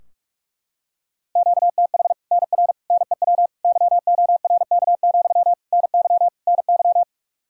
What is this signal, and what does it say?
Morse code, heard as a keyed tone, about 44 wpm, 700 Hz. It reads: QTH NR NEW YORK, NY NY
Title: Practice QSO - Exchange 1 - Beginning - 45wpm